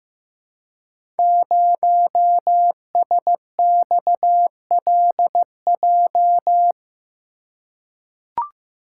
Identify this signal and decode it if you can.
Morse code, heard as a keyed tone, about 15 wpm, 700 Hz. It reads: 0SXLJ
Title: Code Groups - 5 Letter:Numbers - 15wpm